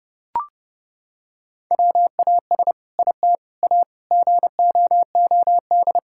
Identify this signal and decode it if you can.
Morse code, heard as a keyed tone, about 32 wpm, 700 Hz. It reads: WAS IT A GOOD
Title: Sentences from Top 1000 Words - Review Part 5 - 30wpm